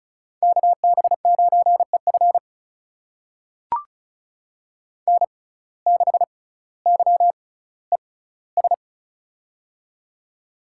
Morse code, heard as a keyed tone, about 36 wpm, spaced as for 18 wpm, 700 Hz. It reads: KB9EF N6YES
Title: USA Call Signs - 5 Characters - 6x Character Spacing - 35wpm